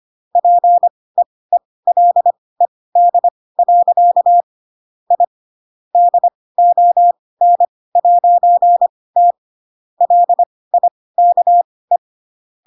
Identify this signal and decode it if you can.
Morse code, heard as a keyed tone, about 25 wpm, spaced as for 18 wpm, 700 Hz. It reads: PEELED. I DON'T LIKE
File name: Pinocchio / Pinocchio ch7 12 25 ik1hge